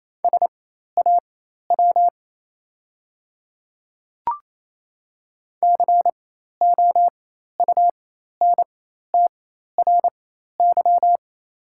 Morse code, heard as a keyed tone, about 28 wpm, spaced as for 13 wpm, 690 Hz. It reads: SAW COUNTRY
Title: 4x Character Spacing - Top 300 Words - 28wpm